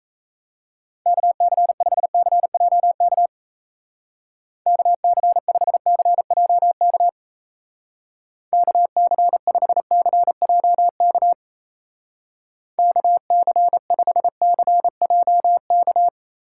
Morse code, heard as a keyed tone, about 36 wpm, 700 Hz. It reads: KC5CJK KC5CJK KC5CJK KC5CJK